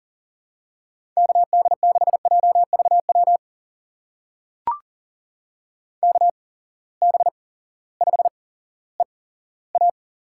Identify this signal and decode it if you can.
Morse code, heard as a keyed tone, about 38 wpm, spaced as for 18 wpm, 700 Hz. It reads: KD6JVW KB5EA
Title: USA Call Signs - 6 Characters - 8x Character Spacing - 40wpm